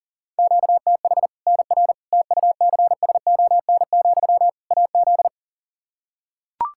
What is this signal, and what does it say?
Morse code, heard as a keyed tone, about 40 wpm, 700 Hz. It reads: QTH NR TUCSON, AZ
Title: K1USN SST QSO Practice 40wpm